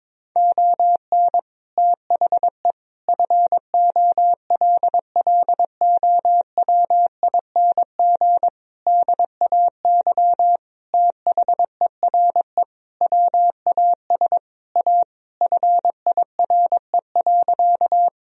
Morse code, heard as a keyed tone, about 22 wpm, 700 Hz. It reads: ON THE FOLLOWING DAY THERE WAS A FIRE.
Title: Sentences from Top 1000 Words - Review Part 4 - 22wpm